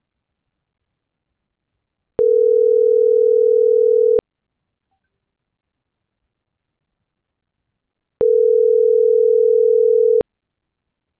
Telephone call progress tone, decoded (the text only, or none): ringback tone